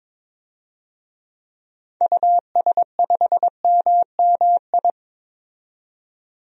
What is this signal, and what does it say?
Morse code, heard as a keyed tone, about 22 wpm, 700 Hz. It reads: US5MMI
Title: DXCC-100 Call Signs - 6 Characters - 22wpm